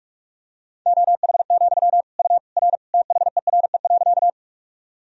Morse code, heard as a keyed tone, about 46 wpm, 700 Hz. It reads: OH, U R THERE.
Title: Sentences from Top 500 Words and Abbreviations 45wpm